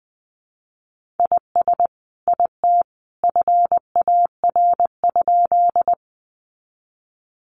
Morse code, heard as a keyed tone, about 20 wpm, 700 Hz. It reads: IS IT FAR?
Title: Sentences from Top 300 Words 20wpm